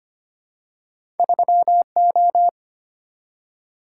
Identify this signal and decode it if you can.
Morse code, heard as a keyed tone, about 25 wpm, 700 Hz. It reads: 3O
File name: Two Letter-Number Combinations - 25wpm